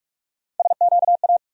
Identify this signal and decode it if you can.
Morse code, heard as a keyed tone, about 46 wpm, 700 Hz. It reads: SQA
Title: USA Call Sign Postfixes - 3 Letters 45wpm